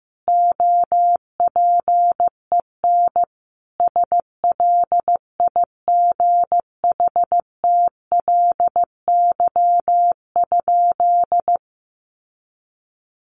Morse code, heard as a keyed tone, about 15 wpm, 700 Hz. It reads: OPEN SLIGHTLY?
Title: Sentences from Top 1300 Words 15wpm